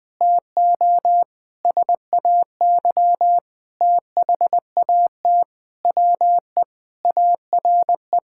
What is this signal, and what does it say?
Morse code, heard as a keyed tone, about 20 wpm, 700 Hz. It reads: TO SAY THAT WE ARE
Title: Sentences with 7-Word Phrases - 20wpm